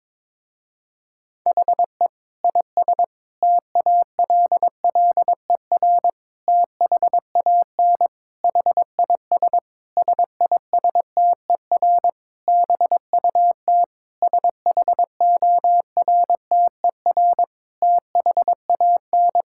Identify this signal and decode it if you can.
Morse code, heard as a keyed tone, about 22 wpm, 700 Hz. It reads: HE IS TALLER THAN HIS SISTER BUT SHORTER THAN
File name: Difficult Sentences with Hints - 22wpm